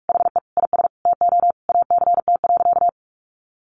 Morse code, heard as a keyed tone, about 44 wpm, 700 Hz. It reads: HE IS TO ACT.
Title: Sentences from Top 1200 Words 45wpm